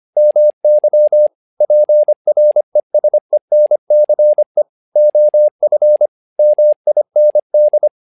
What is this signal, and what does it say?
Morse code, heard as a keyed tone, about 25 wpm, 600 Hz. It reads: MY PRESENCE OF MIND
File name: learning center / tower trouble / tower trouble 25wpm